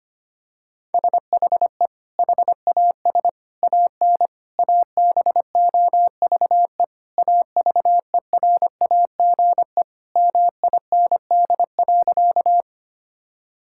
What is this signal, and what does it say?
Morse code, heard as a keyed tone, about 25 wpm, 700 Hz. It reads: SHE HAS AN ABOVE AVERAGE MIND.
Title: Sentences from Top 1000 Words - Review Part 6 - 25wpm